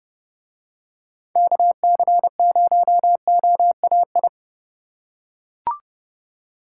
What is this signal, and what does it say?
Morse code, heard as a keyed tone, about 30 wpm, 700 Hz. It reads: KC0OAI